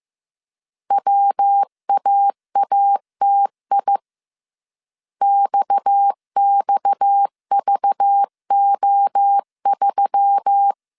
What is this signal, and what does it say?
Morse code, heard as a keyed tone, about 15 wpm, 800 Hz. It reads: WAATI XXVO3
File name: CW / DET / 1 6 1500C S B 14wpm 2958